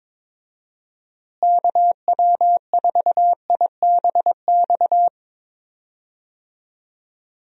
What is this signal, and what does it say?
Morse code, heard as a keyed tone, about 22 wpm, 700 Hz. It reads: KW4IBX
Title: USA Call Signs - 6 Characters - Repeated 3x - 22wpm